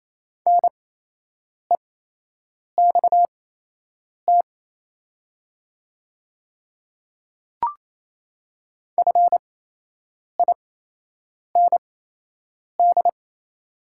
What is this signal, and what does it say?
Morse code, heard as a keyed tone, about 28 wpm, spaced as for 8 wpm, 700 Hz. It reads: NEXT FIND
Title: 8x Character Spacing - Top 300 Words - 28wpm